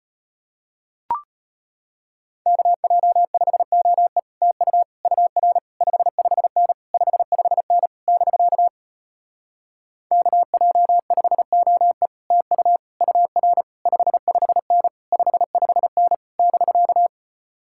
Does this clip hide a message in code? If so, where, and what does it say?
Yes: Morse code from 2.5 to 17.1 s, heard as a keyed tone, about 33 wpm, 700 Hz. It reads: KJ5OE TU UR 55N 55N <BK> KJ5OE TU UR 55N 55N <BK>